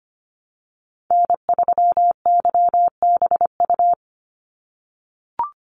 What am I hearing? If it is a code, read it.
Morse code, heard as a keyed tone, about 25 wpm, 700 Hz. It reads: N3YBU
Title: USA Call Signs 25wpm